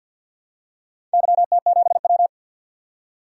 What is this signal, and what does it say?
Morse code, heard as a keyed tone, about 51 wpm, 700 Hz. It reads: YT7W
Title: DXCC-100 Call Signs - 4 Characters - 50wpm